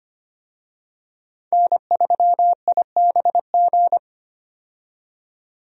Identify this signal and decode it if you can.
Morse code, heard as a keyed tone, about 25 wpm, 700 Hz. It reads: N3IBG
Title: USA Call Signs - 5 Characters - 25wpm